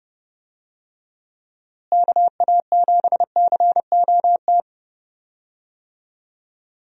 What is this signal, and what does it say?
Morse code, heard as a keyed tone, about 30 wpm, 700 Hz. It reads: KA7COT
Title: USA Call Signs - 6 Characters - Repeated 2x - 30wpm